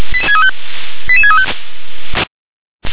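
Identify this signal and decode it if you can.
Five-tone selective call (ZVEI), heard as quick runs of five tones, tones of about 70 ms. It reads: 88635
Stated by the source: ZVEI five-tone selective call received with an SDR.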